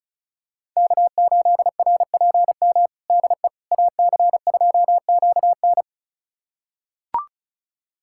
Morse code, heard as a keyed tone, about 34 wpm, 700 Hz. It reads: K8RPM DE AC2QN
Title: USA Callsign DE Callsign - 5 Characters - 35wpm